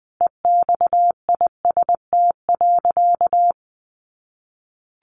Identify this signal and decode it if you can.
Morse code, heard as a keyed tone, about 20 wpm, 700 Hz. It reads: EXIST.
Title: Sentences from Top 1500 Words 20wpm